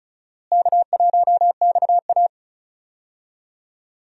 Morse code, heard as a keyed tone, about 34 wpm, 700 Hz. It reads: K1XA
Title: USA Call Signs - 4 Characters - Repeated 2x - 35wpm